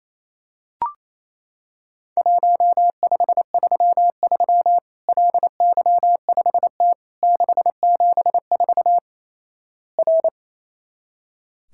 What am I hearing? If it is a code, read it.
Morse code, heard as a keyed tone, about 27 wpm, 700 Hz. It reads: 1533 LY5T 674 R
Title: Worked All Europe DX Contest - QTC Exchange - 28wpm